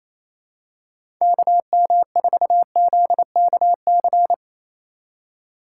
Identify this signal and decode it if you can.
Morse code, heard as a keyed tone, about 28 wpm, 700 Hz. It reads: KM4ZKC
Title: USA Call Signs - 2x Character Spacing - 28wpm